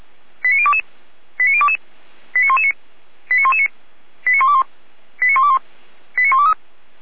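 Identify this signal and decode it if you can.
Five-tone selective call (ZVEI), heard as quick runs of five tones, tones of about 70 ms. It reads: 89022 89022 89119 89119 89121 89121 89123
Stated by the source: ZVEI five-tone selective call 89017 received on SDR.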